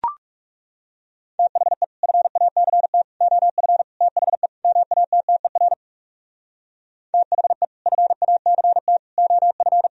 Morse code, heard as a keyed tone, about 40 wpm, 700 Hz. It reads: THE FACT OF THE MATTER THE FACT OF